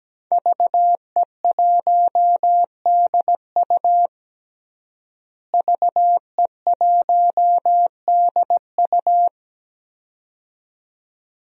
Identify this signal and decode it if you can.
Morse code, heard as a keyed tone, about 17 wpm, 700 Hz. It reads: VE1DU VE1DU
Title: DXCC-100 Call Signs - 5 Characters - Repeated 2x - 17wpm